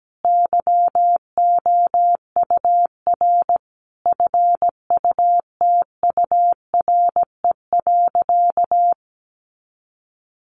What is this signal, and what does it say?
Morse code, heard as a keyed tone, about 17 wpm, 700 Hz. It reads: YOUR FUTURE.
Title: Sentences from Top 1400 Words 17wpm